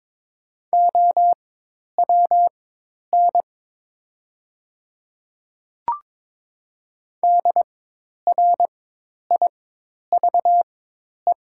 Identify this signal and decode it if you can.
Morse code, heard as a keyed tone, about 22 wpm, spaced as for 10 wpm, 700 Hz. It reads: OWN DRIVE
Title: 4x Character Spacing - Top 500 Words - 22wpm